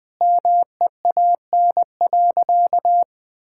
Morse code, heard as a keyed tone, about 20 wpm, 700 Hz. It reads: MEAN.
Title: Sentences from Top 1000 Words - Review Part 6 - 20wpm